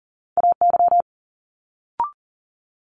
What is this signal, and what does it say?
Morse code, heard as a keyed tone, about 42 wpm, 700 Hz. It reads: AY
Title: Multi-letter Phonemes Quick Fire 40wpm